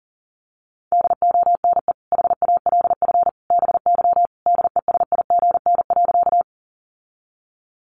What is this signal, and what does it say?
Morse code, heard as a keyed tone, about 40 wpm, 700 Hz. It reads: DONE HALF BY DESIGN.
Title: Sentences from Top 800 Words 40wpm